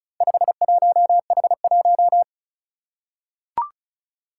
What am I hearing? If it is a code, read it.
Morse code, heard as a keyed tone, about 36 wpm, 690 Hz. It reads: 51H1
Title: Four Challenging Letter-Number Combinations 35wpm